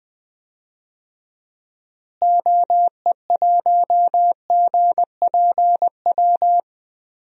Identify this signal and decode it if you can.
Morse code, heard as a keyed tone, about 20 wpm, 700 Hz. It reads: OE1GPW
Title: DXCC-100 Call Signs - 6 Characters - Repeated 3x - 20wpm